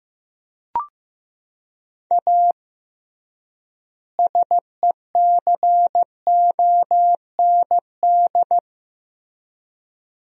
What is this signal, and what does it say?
Morse code, heard as a keyed tone, about 15 wpm, 700 Hz. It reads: A SECOND
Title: 3x Word-Spacing - Sentences from Top 300 Words - 15wpm